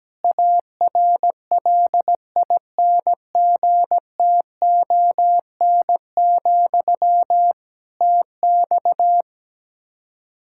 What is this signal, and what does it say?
Morse code, heard as a keyed tone, about 17 wpm, 700 Hz. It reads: ARLINGTON, TX
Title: Practice QSO - Exchange 1 - Beginning - 17wpm